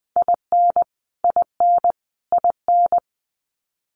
Morse code, heard as a keyed tone, about 20 wpm, 700 Hz. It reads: IN IN IN
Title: Top 500 Words Repeated 20wpm